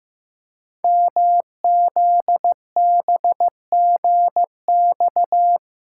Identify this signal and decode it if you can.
Morse code, heard as a keyed tone, about 15 wpm, 700 Hz. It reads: MZBGX
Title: Code Groups - 5 Letters - 15wpm